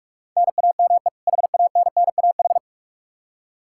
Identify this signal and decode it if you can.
Morse code, heard as a keyed tone, about 46 wpm, 700 Hz. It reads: NAME HANNAH
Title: Practice QSO - Exchange 1 - Beginning - 45wpm